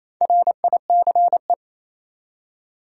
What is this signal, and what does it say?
Morse code, heard as a keyed tone, about 28 wpm, 700 Hz. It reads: RICE
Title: Four Letter Words - 28wpm